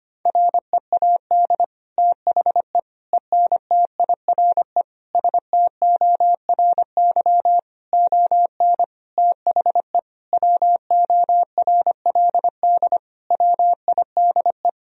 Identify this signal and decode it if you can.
Morse code, heard as a keyed tone, about 25 wpm, 700 Hz. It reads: READ THE ENTIRE STORY ON THE WORLD WIDE
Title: Sentences from Top 1000 Words - Review Part 4 - 25wpm